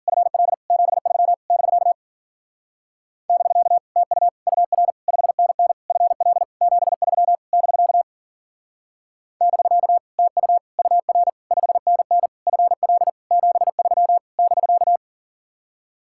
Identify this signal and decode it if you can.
Morse code, heard as a keyed tone, about 48 wpm, 700 Hz. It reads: FL 73 <BK> <BK> TU UR 5NN FL 73 <BK> <BK> TU UR 5NN FL 73 <BK>
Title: Speed-Racing Common QSO Phrases Encore - 40wpm